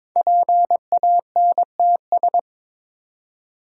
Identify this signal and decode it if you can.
Morse code, heard as a keyed tone, about 22 wpm, 700 Hz. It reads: PANTS
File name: 5 Letter Words - First 3 Letters Given - 22wpm